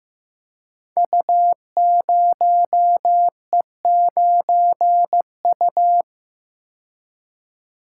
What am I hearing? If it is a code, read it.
Morse code, heard as a keyed tone, about 15 wpm, 700 Hz. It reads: U0E9U
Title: Code Groups - 5 Letter:Numbers - 15wpm